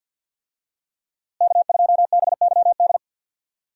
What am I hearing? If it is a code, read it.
Morse code, heard as a keyed tone, about 51 wpm, 700 Hz. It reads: K2BYD
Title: USA Call Signs - 5 Characters - Repeated 2x - 50wpm